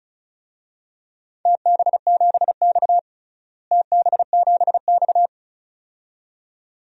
Morse code, heard as a keyed tone, about 36 wpm, 700 Hz. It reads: TB7X TB7X
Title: DXCC-100 Call Signs - 4 Characters - Repeated 2x - 35wpm